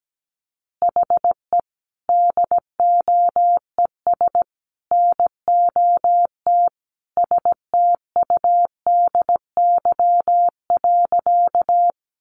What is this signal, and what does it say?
Morse code, heard as a keyed tone, about 17 wpm, 700 Hz. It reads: HE DOES NOT STUDY.
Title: Additional Easy Phrases 17wpm